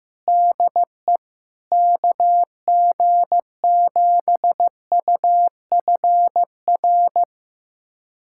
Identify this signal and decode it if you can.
Morse code, heard as a keyed tone, about 15 wpm, 700 Hz. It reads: DE KG7UFR
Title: USA Callsign DE Callsign 15wpm